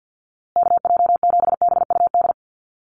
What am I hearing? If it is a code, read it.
Morse code, heard as a keyed tone, about 50 wpm, 700 Hz. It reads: KJ7BUD
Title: USA Call Signs Repeated 2x 50wpm